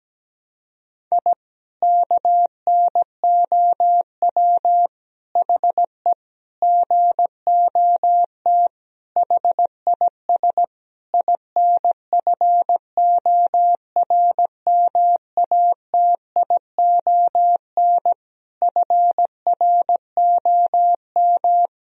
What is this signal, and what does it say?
Morse code, heard as a keyed tone, about 17 wpm, 700 Hz. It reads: I KNOW HE GOT HIS INFORMATION FROM
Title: Sentences from Top 1000 Words - Review Part 3 - 17wpm